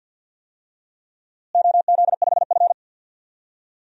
Morse code, heard as a keyed tone, about 50 wpm, 700 Hz. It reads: OZ5F
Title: DXCC-100 Call Signs - 4 Characters - Repeated 3x - 50wpm